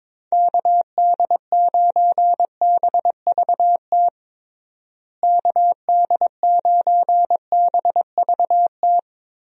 Morse code, heard as a keyed tone, about 22 wpm, 700 Hz. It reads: KD9BVT KD9BVT
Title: USA Call Signs - 6 Characters - Repeated 2x - 22wpm